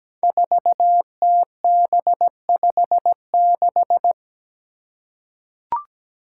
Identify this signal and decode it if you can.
Morse code, heard as a keyed tone, about 17 wpm, 700 Hz. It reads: 4TB56
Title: Five Challenging Letter-Number Combinations 17wpm